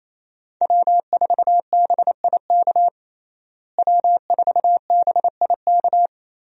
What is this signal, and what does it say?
Morse code, heard as a keyed tone, about 28 wpm, 700 Hz. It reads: W4BIK W4BIK
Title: USA Call Signs - 5 Characters - Repeated 2x - 28wpm